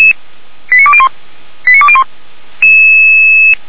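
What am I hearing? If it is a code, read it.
Five-tone selective call (ZVEI), heard as quick runs of five tones, tones of about 70 ms. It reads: 89201 89201
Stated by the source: ZVEI five-tone selective call received by SDR.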